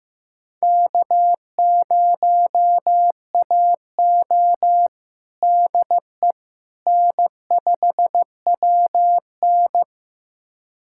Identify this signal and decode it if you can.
Morse code, heard as a keyed tone, about 15 wpm, 700 Hz. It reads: K0AO DE N5WN
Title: USA Callsign DE Callsign - 4 Characters - 15wpm